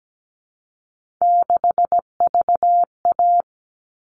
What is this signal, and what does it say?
Morse code, heard as a keyed tone, about 17 wpm, 700 Hz. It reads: 6VA
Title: All Cut Numbers - 3-Digit 17wpm